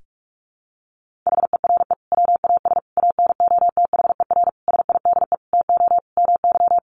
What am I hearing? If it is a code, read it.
Morse code, heard as a keyed tone, about 45 wpm, 700 Hz. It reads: HERE WAS ANOTHER SIDE TO MY
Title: Sentences from Top 1500 Words 45wpm